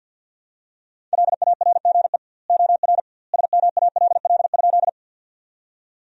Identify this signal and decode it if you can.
Morse code, heard as a keyed tone, about 50 wpm, 700 Hz. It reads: LARGE OR SMALL?
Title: 2x Word-Spacing - Sentences from Top 200 Words - 50wpm